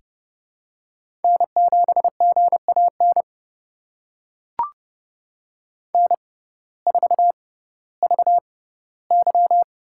Morse code, heard as a keyed tone, about 30 wpm, spaced as for 17 wpm, 700 Hz. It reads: N7GAN N4VY
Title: USA Call Signs - 6x Character Spacing - 30wpm